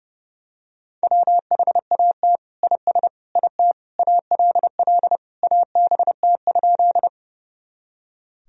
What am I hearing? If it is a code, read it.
Morse code, heard as a keyed tone, about 30 wpm, 700 Hz. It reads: WHAT IS IT ALL ABT?
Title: Sentences from Top 100 Words and Abbreviations 30wpm